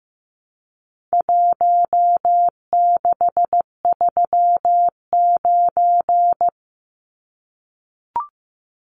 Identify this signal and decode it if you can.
Morse code, heard as a keyed tone, about 15 wpm, 700 Hz. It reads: 1639
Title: Numbers - Contest Style - 4-Digit 15wpm